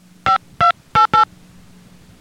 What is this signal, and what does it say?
DTMF, heard as telephone touch tones, tones of about 120 ms. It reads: 5288